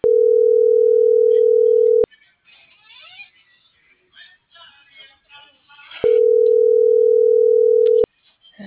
Ringback tone (North America), 440 + 480 Hz.
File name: CENTRAL SAGOT / out diego.emir